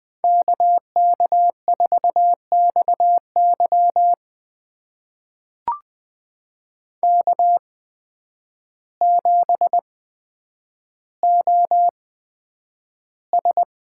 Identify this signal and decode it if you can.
Morse code, heard as a keyed tone, about 20 wpm, spaced as for 9 wpm, 700 Hz. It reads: KK4XY K7OS
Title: USA Call Signs - 5 Characters - 8x Character Spacing - 20wpm